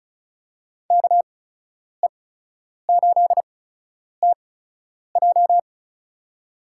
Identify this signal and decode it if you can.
Morse code, heard as a keyed tone, about 36 wpm, spaced as for 10 wpm, 700 Hz. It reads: KE8TJ
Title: USA Call Signs - 5 Characters - 8x Character Spacing - 35wpm